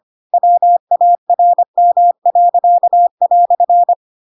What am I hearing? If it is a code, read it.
Morse code, heard as a keyed tone, about 25 wpm, 700 Hz. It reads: WARM."